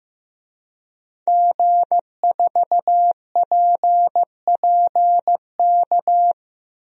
Morse code, heard as a keyed tone, about 16 wpm, 700 Hz. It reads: G4PPK
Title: DXCC-100 Call Signs - 5 Characters - Repeated 2x - 15wpm